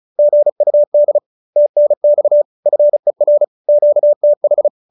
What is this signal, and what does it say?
Morse code, heard as a keyed tone, about 34 wpm, 570 Hz. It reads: GUD TNX FER QTH